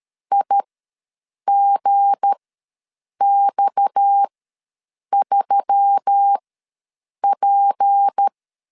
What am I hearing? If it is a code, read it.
Morse code, heard as a keyed tone, about 13 wpm, spaced as for 7 wpm, 800 Hz. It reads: IGX3P
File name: LEZIONI / 7 GF9Z / 7 G ws 12wpm 3140